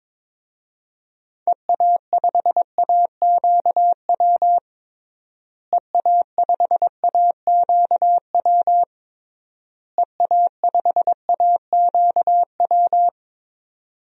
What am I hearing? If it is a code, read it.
Morse code, heard as a keyed tone, about 21 wpm, 700 Hz. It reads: EA5AQW EA5AQW EA5AQW
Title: DXCC-100 Call Signs - 6 Characters - Repeated 3x - 22wpm